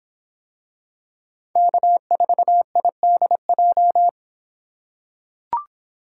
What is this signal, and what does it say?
Morse code, heard as a keyed tone, about 25 wpm, 700 Hz. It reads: K4IDJ